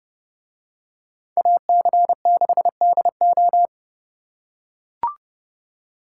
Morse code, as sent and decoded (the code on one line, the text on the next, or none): .- -.-. -.... -.. ---
AC6DO